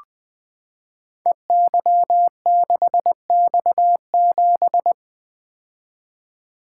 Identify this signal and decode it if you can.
Morse code, heard as a keyed tone, about 20 wpm, 700 Hz. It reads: EY6X7
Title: Code Groups - 5 Letter:Numbers - 20wpm